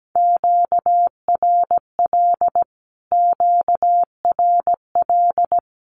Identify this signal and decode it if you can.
Morse code, heard as a keyed tone, about 17 wpm, 700 Hz. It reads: QRL QRL
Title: Q-Codes Repeated 3x 17wpm